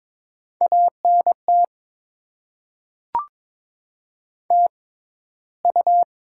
Morse code, heard as a keyed tone, about 22 wpm, spaced as for 14 wpm, 700 Hz. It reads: ANT TU
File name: Top 20 QSO Elements - 6x Character Spacing - 22wpm